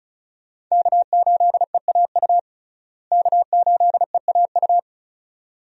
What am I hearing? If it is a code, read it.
Morse code, heard as a keyed tone, about 34 wpm, 700 Hz. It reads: K8EAU K8EAU
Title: USA Call Signs - 5 Characters - Repeated 2x - 35wpm